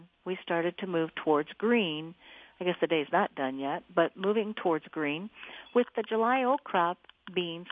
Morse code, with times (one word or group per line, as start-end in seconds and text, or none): none